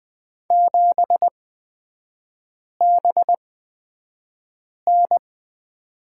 Morse code, heard as a keyed tone, about 20 wpm, spaced as for 5 wpm, 700 Hz. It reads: 7BN
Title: 250 Lesson 40e - BK - Three Characters 20wpm